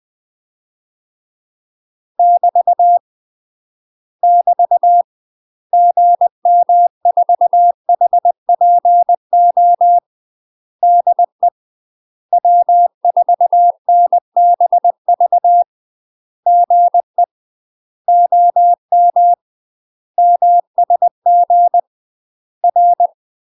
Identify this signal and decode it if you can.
Morse code, heard as a keyed tone, about 20 wpm, 700 Hz. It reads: <BT> <BT> GM4HPO DE W4NBV GE OM MSG R
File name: UK MASTER QSOs TRAINING / 20WPM / UK MASTER QSOs 0013